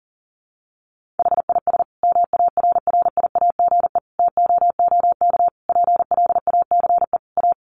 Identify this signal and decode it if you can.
Morse code, heard as a keyed tone, about 40 wpm, 700 Hz. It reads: HIS MARRIAGE TOOK PLACE A
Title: Sentences from Top 1500 Words 40wpm